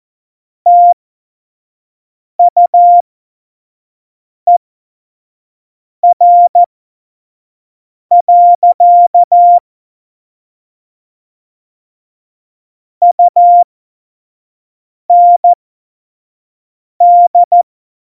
Morse code, heard as a keyed tone, about 14 wpm, spaced as for 5 wpm, 700 Hz. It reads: TUER. UND
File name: Morsekurs / Witze / Witz 13. Oktober 2020, 25 F70